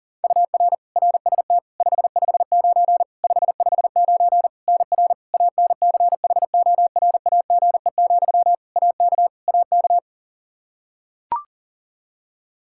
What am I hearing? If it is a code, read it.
Morse code, heard as a keyed tone, about 40 wpm, 700 Hz. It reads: UR RST 559 559 NR ANCHORAGE, AK AK
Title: Practice QSO - Exchange 1 - Beginning - 40wpm